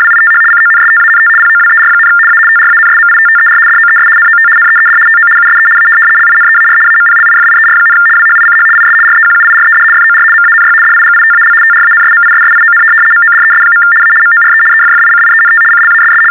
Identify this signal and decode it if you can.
RTTY, about 75 baud. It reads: WITH CONSTANT POWER, A TRANSMITTER DOES NOT NEED TO USE A LINEAR AMPLIFIER, WHICH IS REQUIRED FOR MANY DIGITAL TRANSMISSION MODES. A MORE EFFICIENT CLASS
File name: teletype-RTTY-75